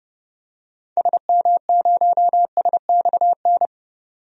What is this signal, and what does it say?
Morse code, heard as a keyed tone, about 30 wpm, 700 Hz. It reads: SM0SXN
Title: DXCC-100 Call Signs - 6 Characters - 30wpm